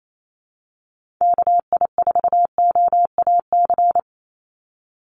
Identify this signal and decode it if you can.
Morse code, heard as a keyed tone, about 28 wpm, 700 Hz. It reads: KI4OAC
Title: USA Call Signs 28wpm